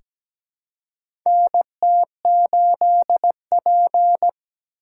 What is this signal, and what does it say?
Morse code, heard as a keyed tone, about 17 wpm, 700 Hz. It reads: NT8P
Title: USA Call Signs - 4 Characters - 4x Character Spacing - 17wpm